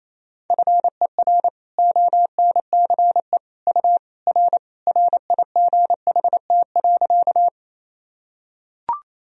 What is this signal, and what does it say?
Morse code, heard as a keyed tone, about 28 wpm, 700 Hz. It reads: FER ONCE U R RIGHT.
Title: Sentences from Top 300 Words and Abbreviations 28wpm